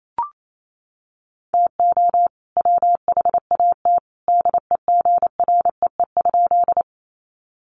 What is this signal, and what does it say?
Morse code, heard as a keyed tone, about 28 wpm, 700 Hz. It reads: TO WHAT DEGREE?
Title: Sentences from Top 1500 Words 28wpm